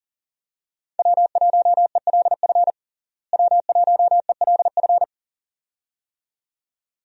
Morse code, heard as a keyed tone, about 40 wpm, 700 Hz. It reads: W1ELF W1ELF
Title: USA Call Signs - 5 Characters - Repeated 2x - 40wpm